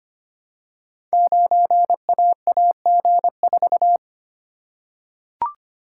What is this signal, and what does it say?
Morse code, heard as a keyed tone, about 25 wpm, 700 Hz. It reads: 9AAG4
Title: Code Groups - 5 Letter:Numbers - 25wpm